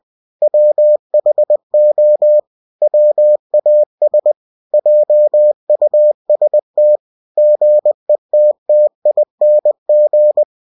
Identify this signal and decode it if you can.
Morse code, heard as a keyed tone, about 20 wpm, 600 Hz. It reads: WHO WAS JUST GETTING